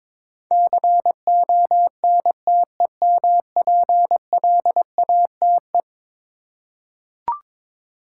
Morse code, heard as a keyed tone, about 22 wpm, 700 Hz. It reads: CONTEMPLATE
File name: practice / 22wpm / Eleven Letter Words - 22wpm